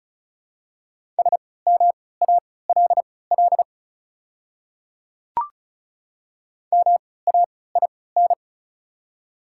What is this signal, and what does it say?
Morse code, heard as a keyed tone, about 34 wpm, spaced as for 20 wpm, 690 Hz. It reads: SMALL MAIN
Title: 3x Character Spacing - Top 500 Words - 35wpm